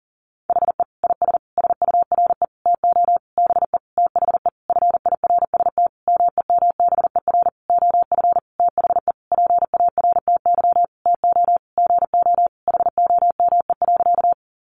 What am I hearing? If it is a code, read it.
Morse code, heard as a keyed tone, about 40 wpm, 700 Hz. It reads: HE IS SURE TO BE THE FIRST MEMBER OF THE PARTY TO GO HOME.
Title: Sentences from Top 1400 Words 40wpm